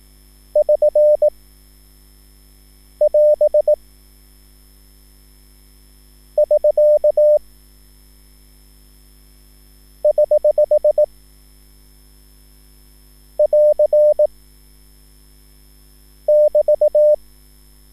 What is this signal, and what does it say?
Morse code, heard as a keyed tone, about 18 wpm, 600 Hz. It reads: <SN> <AS> <SK> <HH> <AR> <BT>